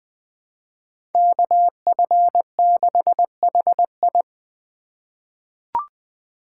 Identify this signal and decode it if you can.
Morse code, heard as a keyed tone, about 20 wpm, 700 Hz. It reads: KF6HI
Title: 2023 SOTA Chaser Call Signs - 20wpm